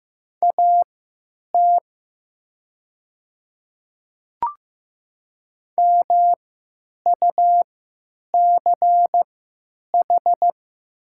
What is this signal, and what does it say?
Morse code, heard as a keyed tone, about 16 wpm, spaced as for 9 wpm, 700 Hz. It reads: AT MUCH
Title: 3x Character Spacing - Top 200 Words - 15wpm